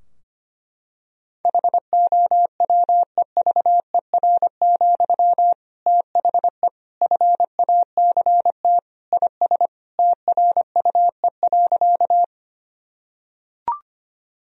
Morse code, heard as a keyed tone, about 25 wpm, 700 Hz. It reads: HOWEVER, THE FACT IS TRUE.
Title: Sentences from Top 1000 Words - Review Part 3 - 25wpm